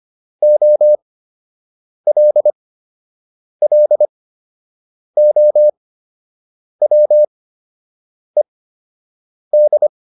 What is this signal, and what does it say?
Morse code, heard as a keyed tone, about 25 wpm, spaced as for 7 wpm, 600 Hz. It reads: OLLOWED